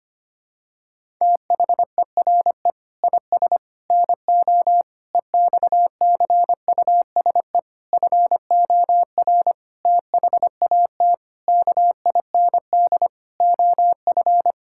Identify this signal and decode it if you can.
Morse code, heard as a keyed tone, about 25 wpm, 700 Hz. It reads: THERE IS NO EXCUSE FOR THAT KIND OF
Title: Sentences from Top 1000 Words - Review Part 6 - 25wpm